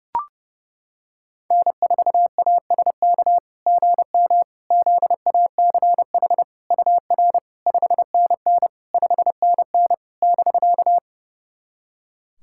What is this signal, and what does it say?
Morse code, heard as a keyed tone, about 30 wpm, 660 Hz. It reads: N4ASK GM ZACH UR 5NN 5NN <BK>
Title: SOTA QSO Practice 30wpm